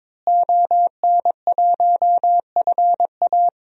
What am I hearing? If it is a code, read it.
Morse code, heard as a keyed tone, about 22 wpm, 700 Hz. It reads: ON1FA
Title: DXCC-100 Call Signs - 5 Characters - Repeated 2x - 22wpm